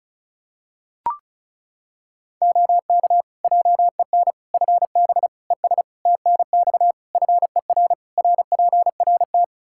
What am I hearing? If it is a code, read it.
Morse code, heard as a keyed tone, about 36 wpm, 700 Hz. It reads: OK JEN FB ES TNX FER RPRT
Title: Practice QSO - Exchange 2 - Info - 35wpm